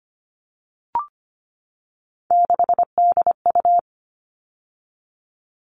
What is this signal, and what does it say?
Morse code, heard as a keyed tone, about 25 wpm, 700 Hz. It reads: 6DU
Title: All Cut Numbers - 3-Digit 25wpm